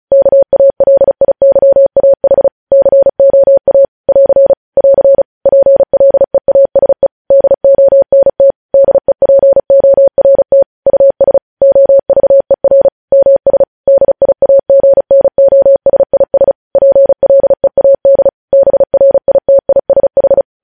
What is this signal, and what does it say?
Morse code, heard as a keyed tone, about 36 wpm, 550 Hz. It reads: KALIYAH COA <AR> <AR> PLEASE DONT DEPORT US OVER MS DIAGNOSIS PLEAD BRITISH
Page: news headlines in morse code at 35 wpm